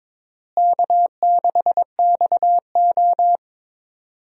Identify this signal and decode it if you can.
Morse code, heard as a keyed tone, about 22 wpm, 700 Hz. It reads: K6XO
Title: USA Call Signs - 4 Characters - Repeated 3x - 22wpm